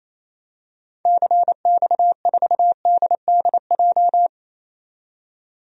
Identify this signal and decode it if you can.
Morse code, heard as a keyed tone, about 27 wpm, 700 Hz. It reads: CX4DDJ
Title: DXCC-100 Call Signs - 6 Characters - 28wpm